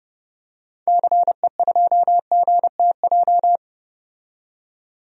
Morse code, heard as a keyed tone, about 31 wpm, 700 Hz. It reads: CE2GTJ
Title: DXCC-100 Call Signs - 6 Characters - 30wpm